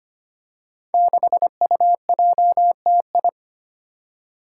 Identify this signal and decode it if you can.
Morse code, heard as a keyed tone, about 25 wpm, 700 Hz. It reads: 6UJTI
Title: Five Challenging Letter-Number Combinations 25wpm